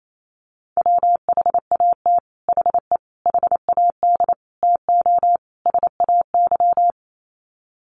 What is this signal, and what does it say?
Morse code, heard as a keyed tone, about 28 wpm, 700 Hz. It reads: WHAT HE HAD TO SAY
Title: Sets of 5 Words Encore2 28wpm